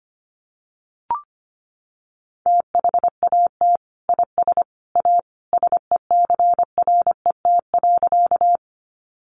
Morse code, heard as a keyed tone, about 25 wpm, 700 Hz. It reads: THAT IS A SECRET.
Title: Sentences from Top 1200 Words 25wpm